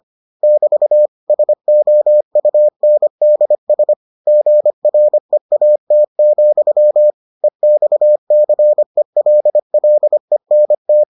Morse code, heard as a keyed tone, about 25 wpm, 600 Hz. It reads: <BT> SOUNDS GREAT, EXCELLENT